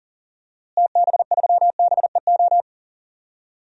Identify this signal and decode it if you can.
Morse code, heard as a keyed tone, about 37 wpm, 700 Hz. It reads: TB3BEO
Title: DXCC-100 Call Signs - 6 Characters - Repeated 2x - 40wpm